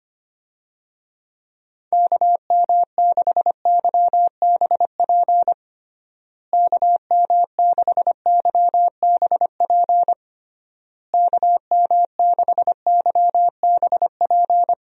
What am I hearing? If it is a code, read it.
Morse code, heard as a keyed tone, about 25 wpm, 700 Hz. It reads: KM6YBP KM6YBP KM6YBP
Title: USA Call Signs - 6 Characters - Repeated 3x - 25wpm